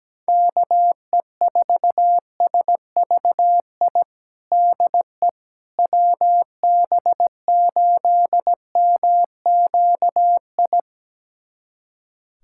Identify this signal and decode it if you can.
Morse code, heard as a keyed tone, about 17 wpm, 700 Hz. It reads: KE4SVI DE WB8MQI
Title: USA Callsign DE Callsign - 6 Characters - 17wpm